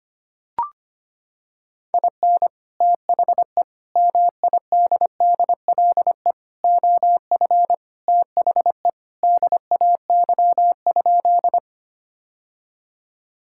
Morse code, heard as a keyed tone, about 25 wpm, 700 Hz. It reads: IN THE MIDDLE OF THE DAY?
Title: Sentences from Top 1000 Words - Review Part 6 - 25wpm